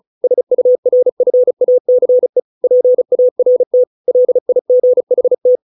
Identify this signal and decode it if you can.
Morse code, heard as a keyed tone, about 34 wpm, 500 Hz. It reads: SURFACE PART LIGHT
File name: 632 words / 632 Words 35WPM